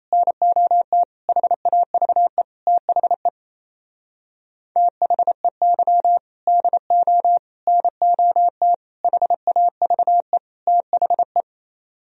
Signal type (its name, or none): Morse code